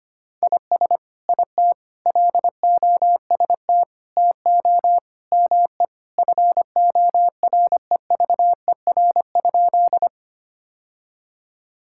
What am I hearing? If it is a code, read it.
Morse code, heard as a keyed tone, about 25 wpm, 700 Hz. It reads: IS IT LOST TO ME FOREVER?
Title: Sentences from Top 1600 Words 25wpm